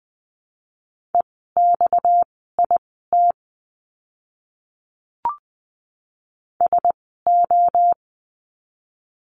Morse code, heard as a keyed tone, about 20 wpm, spaced as for 14 wpm, 700 Hz. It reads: EXIT SO
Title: 230 Lesson 37f - Letter x - Words 20wpm